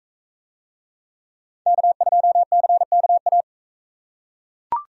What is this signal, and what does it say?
Morse code, heard as a keyed tone, about 41 wpm, 700 Hz. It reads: K2CKA